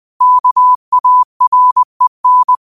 Morse code, heard as a keyed tone, about 20 wpm, 1000 Hz. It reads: KAREN
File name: qotdfiles / 20wpm